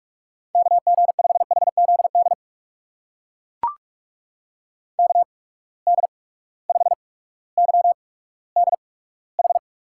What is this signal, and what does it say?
Morse code, heard as a keyed tone, about 42 wpm, spaced as for 19 wpm, 700 Hz. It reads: KG5HZD KD5YDH
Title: USA Call Signs - 6 Characters - 8x Character Spacing - 45wpm